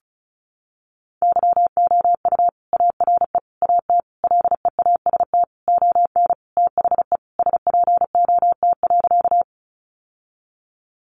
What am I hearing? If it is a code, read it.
Morse code, heard as a keyed tone, about 34 wpm, 700 Hz. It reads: YOU ARE AT LEAST ON THE SPOT.
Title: Sentences from Top 1100 Words 35wpm